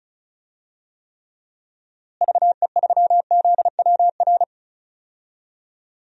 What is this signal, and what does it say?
Morse code, heard as a keyed tone, about 36 wpm, 700 Hz. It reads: VE3ZWR
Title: DXCC-100 Call Signs - 6 Characters - Repeated 3x - 35wpm